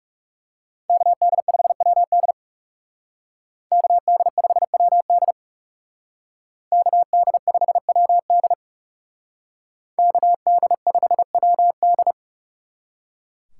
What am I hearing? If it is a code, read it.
Morse code, heard as a keyed tone, about 39 wpm, 700 Hz. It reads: KD5WD KD5WD KD5WD KD5WD